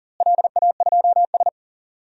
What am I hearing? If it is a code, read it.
Morse code, heard as a keyed tone, about 41 wpm, 700 Hz. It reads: LA2S
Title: DXCC-100 Call Signs - 4 Characters - Repeated 2x - 40wpm